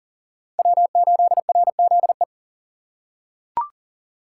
Morse code, heard as a keyed tone, about 40 wpm, 700 Hz. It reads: W8RZE
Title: USA Call Signs - 5 Characters - 3x Character Spacing - 40wpm